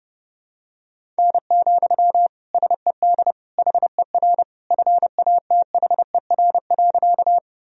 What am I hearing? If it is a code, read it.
Morse code, heard as a keyed tone, about 32 wpm, 700 Hz. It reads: N, SED HER FATHER.
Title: Sentences from Top 300 Words and Abbreviations 30wpm